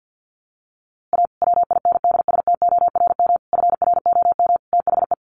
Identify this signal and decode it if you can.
Morse code, heard as a keyed tone, about 49 wpm, 700 Hz. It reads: A WINDSTORM FROM THE
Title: A Sound Of Thunder 50wpm